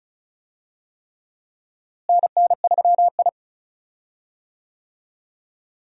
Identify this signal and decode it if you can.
Morse code, heard as a keyed tone, about 36 wpm, 700 Hz. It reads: NN3I
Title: USA Call Signs - 4 Characters - Repeated 3x - 35wpm